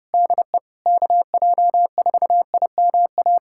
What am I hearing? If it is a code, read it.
Morse code, heard as a keyed tone, about 30 wpm, 700 Hz. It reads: DE KJ4IMA
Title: USA Callsign DE Callsign - 6 Characters - 30wpm